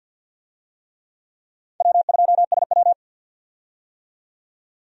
Morse code, heard as a keyed tone, about 47 wpm, 700 Hz. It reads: W2SW
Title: USA Call Signs - 4 Characters - 50wpm